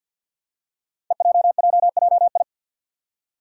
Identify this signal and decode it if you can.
Morse code, heard as a keyed tone, about 47 wpm, 700 Hz. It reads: EJJJI
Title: Five Challenging Letter-Number Combinations 50wpm